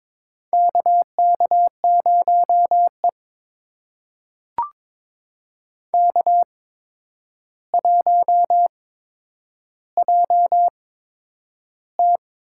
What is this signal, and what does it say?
Morse code, heard as a keyed tone, about 22 wpm, spaced as for 9 wpm, 700 Hz. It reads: KK0E K1JT
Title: USA Call Signs - 4 Characters - 8x Character Spacing - 22wpm